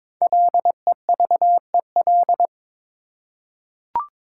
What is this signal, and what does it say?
Morse code, heard as a keyed tone, about 22 wpm, 700 Hz. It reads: LEVEL
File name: Difficult Words with Hints - 22wpm